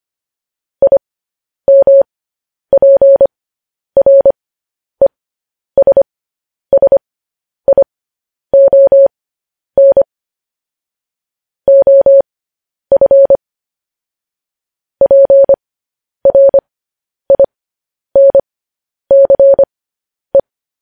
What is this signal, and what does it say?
Morse code, heard as a keyed tone, about 25 wpm, spaced as for 10 wpm, 550 Hz. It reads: IMPRESSION OF PRINCE
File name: podcast / News headlines-1x-25x10